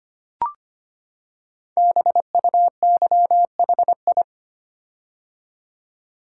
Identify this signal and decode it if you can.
Morse code, heard as a keyed tone, about 25 wpm, 700 Hz. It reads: BUYHI
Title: Code Groups - 5 Letters - 25wpm